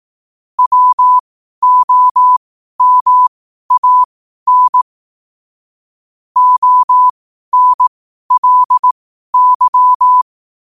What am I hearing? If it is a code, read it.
Morse code, heard as a keyed tone, about 18 wpm, spaced as for 12 wpm, 1000 Hz. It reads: WOMAN ONLY
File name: qotdfiles / 12wpm